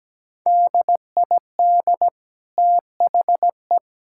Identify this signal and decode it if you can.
Morse code, heard as a keyed tone, about 17 wpm, 700 Hz. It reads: DID THE
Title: Sentences from Top 1000 Words - Review Part 5 - 17wpm